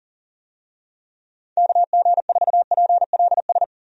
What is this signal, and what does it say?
Morse code, heard as a keyed tone, about 40 wpm, 700 Hz. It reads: KG4PLS
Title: USA Call Signs - 6 Characters - Repeated 3x - 40wpm